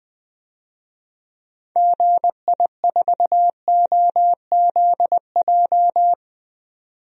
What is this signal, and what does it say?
Morse code, heard as a keyed tone, about 20 wpm, 700 Hz. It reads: GI4OZJ
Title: DXCC-100 Call Signs - 6 Characters - 20wpm